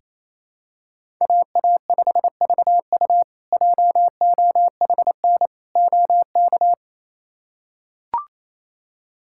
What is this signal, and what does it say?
Morse code, heard as a keyed tone, about 28 wpm, 700 Hz. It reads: AA5VU JOHN OK
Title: K1USN SST QSO Practice - 28wpm